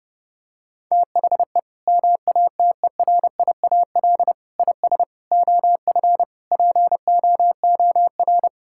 Morse code, heard as a keyed tone, about 30 wpm, 700 Hz. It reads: THE MATERIAL IS OF POOR
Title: Sentences from Top 1000 Words - Review Part 5 - 30wpm